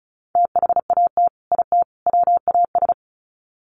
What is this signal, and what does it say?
Morse code, heard as a keyed tone, about 34 wpm, 700 Hz. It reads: THAT IT WAS
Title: Sets of 3 Words 35wpm